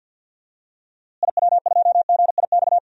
Morse code, heard as a keyed tone, about 46 wpm, 700 Hz. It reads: IW2GIX
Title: DXCC-100 Call Signs - 6 Characters - Repeated 2x - 50wpm